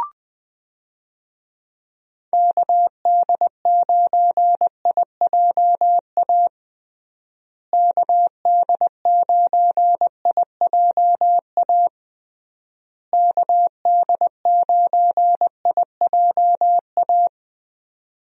Morse code, heard as a keyed tone, about 20 wpm, 700 Hz. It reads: KD9IJA KD9IJA KD9IJA
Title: USA Call Signs - 6 Characters - Repeated 3x - 20wpm